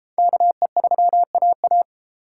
Morse code, heard as a keyed tone, about 30 wpm, 700 Hz. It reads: KE3AA